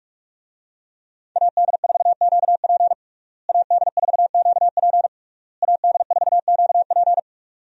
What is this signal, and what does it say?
Morse code, heard as a keyed tone, about 41 wpm, 700 Hz. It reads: AD4QP AD4QP AD4QP
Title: USA Call Signs - 5 Characters - Repeated 3x - 45wpm